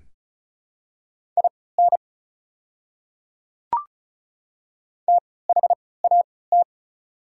Morse code, heard as a keyed tone, about 36 wpm, spaced as for 20 wpm, 690 Hz. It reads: IN THAT
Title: 3x Character Spacing - Top 100 Words - 35wpm